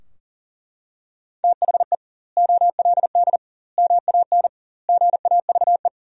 Morse code, heard as a keyed tone, about 40 wpm, 700 Hz. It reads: THE OLD MAN GAVE
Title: 2x Word-Spacing - Sentences from Top 500 Words - 40wpm